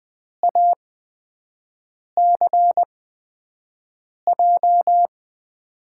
Morse code, heard as a keyed tone, about 20 wpm, spaced as for 5 wpm, 700 Hz. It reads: ACJ
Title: USA Call Signs - 6 Characters - 8x Character Spacing - 20wpm